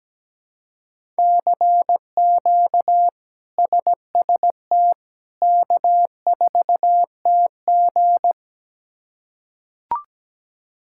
Morse code, heard as a keyed tone, about 17 wpm, 700 Hz. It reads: CQ SST K4TG
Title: K1USN SST QSO Practice - 17wpm